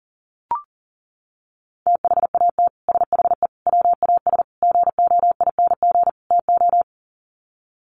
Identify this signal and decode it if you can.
Morse code, heard as a keyed tone, about 40 wpm, 700 Hz. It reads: THAT SHE WAS GOING TO
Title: Sets of 5 Words Encore2 40wpm